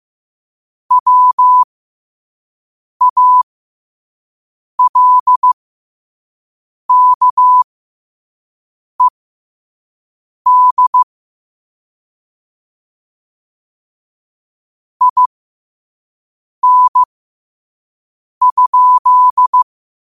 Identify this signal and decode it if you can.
Morse code, heard as a keyed tone, about 15 wpm, spaced as for 5 wpm, 1000 Hz. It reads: WALKED IN?
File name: 5wpm